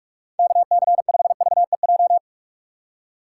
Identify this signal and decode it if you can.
Morse code, heard as a keyed tone, about 44 wpm, 700 Hz. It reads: K C 5 V E J
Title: USA Call Signs - 6 Characters - 6x Character Spacing - 45wpm